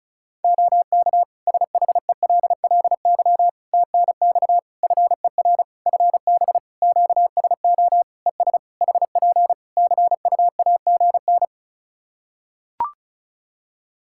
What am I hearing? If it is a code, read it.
Morse code, heard as a keyed tone, about 34 wpm, 700 Hz. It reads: OK SHELLY TNX FER FB QSO ES HP CUAGN
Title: Practice QSO - Exchange 4 - Ending - 35wpm